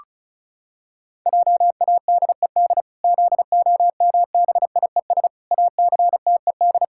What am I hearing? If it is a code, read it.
Morse code, heard as a keyed tone, about 34 wpm, 700 Hz. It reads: JADED ZOMBIES ACTED
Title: Pangram Challenge v2 - 35wpm